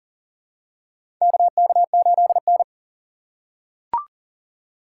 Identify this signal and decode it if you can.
Morse code, heard as a keyed tone, about 40 wpm, 700 Hz. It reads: KK8N
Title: USA Call Signs - 4 Characters - 6x Character Spacing - 40wpm